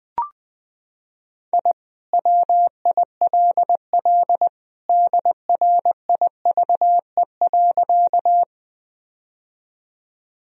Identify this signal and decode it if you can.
Morse code, heard as a keyed tone, about 20 wpm, 700 Hz. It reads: I WILL DRIVE.
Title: Sentences from Top 1000 Words - Review Part 2 - 20wpm